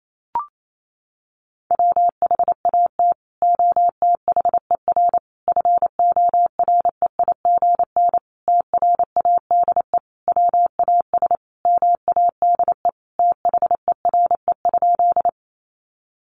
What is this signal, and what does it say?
Morse code, heard as a keyed tone, about 28 wpm, 700 Hz. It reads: WHAT OTHER FOREIGN TRADE WAS MADE THERE?
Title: Sentences from Top 1400 Words 28wpm